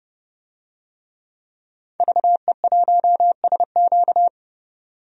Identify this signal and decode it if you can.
Morse code, heard as a keyed tone, about 30 wpm, 700 Hz. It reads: VE1SQ
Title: DXCC-100 Call Signs - 5 Characters - 30wpm